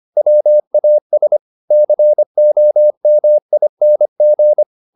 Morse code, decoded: WAS COMING

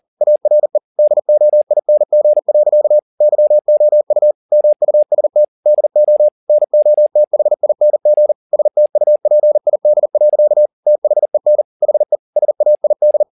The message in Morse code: ARE DOING. YOU MUST DO NOTHING STUPID. THEN HE SAID

